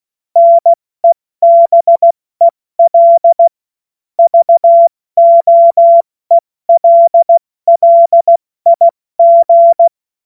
Morse code, heard as a keyed tone, about 16 wpm, spaced as for 14 wpm, 680 Hz. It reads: NEBEL VOELLIG
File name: Morsekurs / Witze / Witz 16. August 70 F80